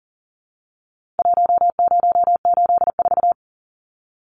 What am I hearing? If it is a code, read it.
Morse code, heard as a keyed tone, about 40 wpm, 700 Hz. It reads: 1084
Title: Numbers - Contest Style - 4-Digit 40wpm